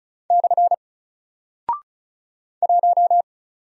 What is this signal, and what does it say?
Morse code, heard as a keyed tone, about 36 wpm, 700 Hz. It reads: / 1
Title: Single Character (Letter:Number:Punctuation:Prosign) 35wpm